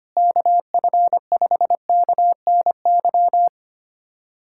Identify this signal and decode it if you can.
Morse code, heard as a keyed tone, about 25 wpm, 700 Hz. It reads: KF5KNY
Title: USA Callsign DE Callsign - 6 Characters - 25wpm